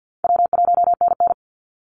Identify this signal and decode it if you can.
Morse code, heard as a keyed tone, about 49 wpm, 700 Hz. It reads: W1NN
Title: Easy Call Signs Repeated 3x - 50wpm